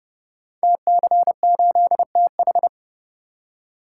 Morse code, heard as a keyed tone, about 30 wpm, 700 Hz. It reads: TC8TH
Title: Code Groups - 5 Letter:Numbers - 30wpm